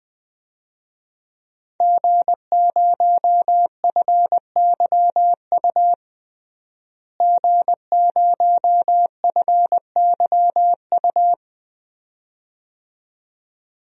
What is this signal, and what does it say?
Morse code, heard as a keyed tone, about 21 wpm, 700 Hz. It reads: G0FYU G0FYU
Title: DXCC-100 Call Signs - 5 Characters - Repeated 2x - 20wpm